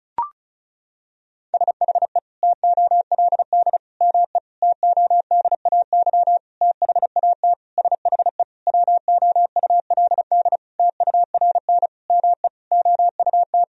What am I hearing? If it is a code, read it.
Morse code, heard as a keyed tone, about 36 wpm, 700 Hz. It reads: SHE TOLD ME TODAY THAT SHE WOULD TURN ME OUT
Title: Sentences from Top 1000 Words - Review Part 4 - 35wpm